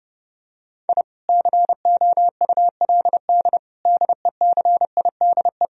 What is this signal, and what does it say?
Morse code, heard as a keyed tone, about 30 wpm, 700 Hz. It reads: I COULD DECIDE
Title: Sentences from Top 1000 Words - Review Part 3 - 30wpm